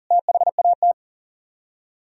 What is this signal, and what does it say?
Morse code, heard as a keyed tone, about 52 wpm, 700 Hz. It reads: THAT